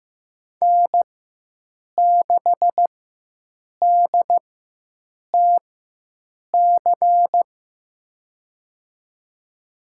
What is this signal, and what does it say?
Morse code, heard as a keyed tone, about 15 wpm, spaced as for 7 wpm, 700 Hz. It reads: N6DTC
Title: USA Call Signs - 5 Characters - 4x Character Spacing - 15wpm